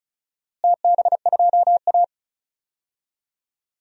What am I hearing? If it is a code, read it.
Morse code, heard as a keyed tone, about 36 wpm, 700 Hz. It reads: TB2A
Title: DXCC-100 Call Signs - 4 Characters - Repeated 3x - 35wpm